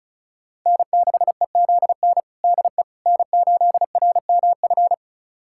Morse code, heard as a keyed tone, about 36 wpm, 700 Hz. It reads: N6EZN DE N8RMF
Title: USA Callsign DE Callsign - 5 Characters - 35wpm